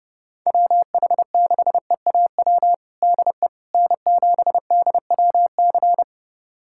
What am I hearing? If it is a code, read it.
Morse code, heard as a keyed tone, about 32 wpm, 700 Hz. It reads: WH6EAW DE N7DWC
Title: USA Callsign DE Callsign 30wpm